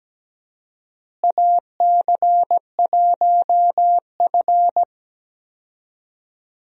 Morse code, heard as a keyed tone, about 17 wpm, 700 Hz. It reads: AC1F
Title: USA Call Signs - 4 Characters - Repeated 2x - 17wpm